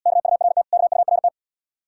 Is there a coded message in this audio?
Yes: Morse code from 0.1 to 1.3 s, heard as a keyed tone, about 80 wpm, 700 Hz. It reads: KURT KURT